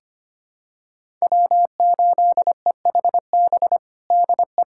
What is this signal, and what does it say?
Morse code, heard as a keyed tone, about 25 wpm, 700 Hz. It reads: W8EHB DE
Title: USA Callsign DE Callsign - 5 Characters - 25wpm